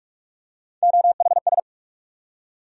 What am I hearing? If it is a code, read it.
Morse code, heard as a keyed tone, about 44 wpm, 700 Hz. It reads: OHS
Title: USA Call Sign Postfixes - 3 Letters 45wpm